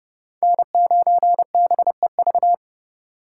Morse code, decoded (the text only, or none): N9BEV